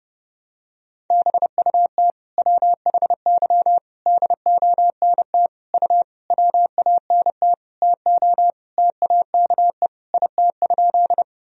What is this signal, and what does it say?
Morse code, heard as a keyed tone, about 30 wpm, 700 Hz. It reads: BUT WHY DONT U WANT TO TAKE IT?
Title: Sentences from Top 300 Words and Abbreviations 30wpm